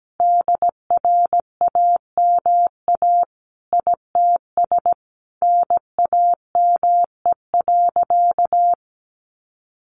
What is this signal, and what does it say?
Morse code, heard as a keyed tone, about 17 wpm, 700 Hz. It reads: DRAMA ITS NAME.
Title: Sentences from Top 1500 Words 17wpm